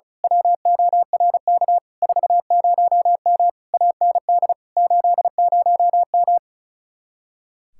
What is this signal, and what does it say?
Morse code, heard as a keyed tone, about 32 wpm, 700 Hz. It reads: WORK 40M AND 80M
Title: Speed-Racing Common QSO Phrases - 35wpm